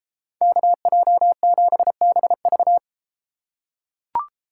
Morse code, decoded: KJ7BV

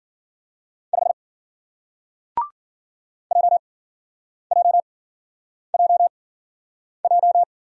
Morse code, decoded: 5 J J J J

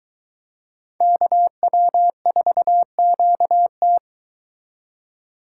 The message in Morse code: KW4QT

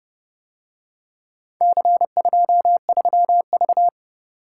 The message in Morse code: C23V

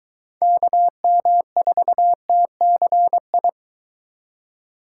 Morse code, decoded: KM4TCI

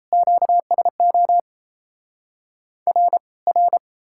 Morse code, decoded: QSO R R

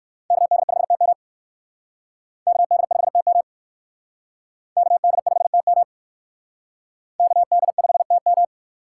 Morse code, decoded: KD5TM KD5TM KD5TM KD5TM

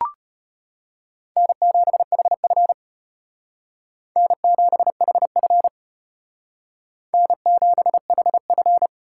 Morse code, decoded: N7HF N7HF N7HF

